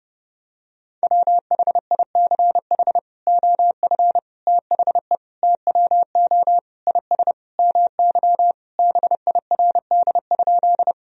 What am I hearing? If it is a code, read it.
Morse code, heard as a keyed tone, about 30 wpm, 700 Hz. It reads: WHICH OF THE TWO IS MY BIRD?
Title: Sentences from Top 1000 Words - Review Part 2 - 30wpm